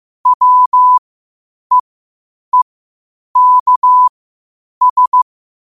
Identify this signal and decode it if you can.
Morse code, heard as a keyed tone, about 15 wpm, spaced as for 8 wpm, 1000 Hz. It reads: WEEKS
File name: qotdfiles / 8wpm